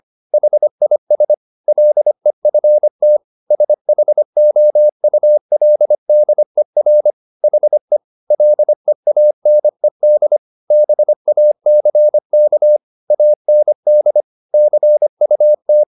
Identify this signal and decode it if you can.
Morse code, heard as a keyed tone, about 25 wpm, 600 Hz. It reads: HIS LEFT SHOULDER HE LEANED BACK AND CUT
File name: the old man and the sea part 1 25wpm